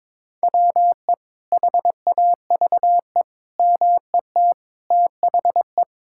Morse code, heard as a keyed tone, about 22 wpm, 700 Hz. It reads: WE HAVE MET THE
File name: Sentences with 9-Word Phrases - 22wpm